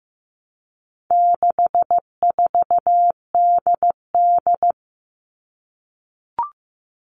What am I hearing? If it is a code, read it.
Morse code, heard as a keyed tone, about 15 wpm, 700 Hz. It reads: 64DD
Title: All Cut Numbers - 4-Digit 15wpm